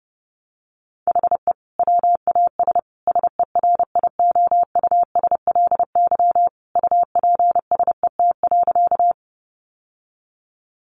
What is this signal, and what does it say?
Morse code, heard as a keyed tone, about 28 wpm, 700 Hz. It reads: HE WAS SERIOUSLY UPSET.
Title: Sentences from Top 1200 Words 30wpm